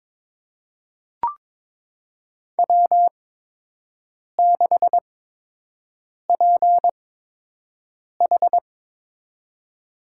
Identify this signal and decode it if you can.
Morse code, heard as a keyed tone, about 22 wpm, spaced as for 6 wpm, 700 Hz. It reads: W6PH
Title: USA Call Signs - 4 Characters - 8x Character Spacing - 22wpm